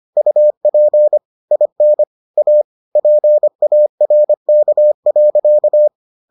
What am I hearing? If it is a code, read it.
Morse code, heard as a keyed tone, about 25 wpm, 600 Hz. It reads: UP IN A PARK.